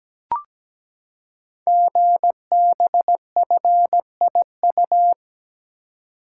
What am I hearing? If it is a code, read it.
Morse code, heard as a keyed tone, about 17 wpm, 700 Hz. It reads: GBFIU
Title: Code Groups - 5 Letters - 17wpm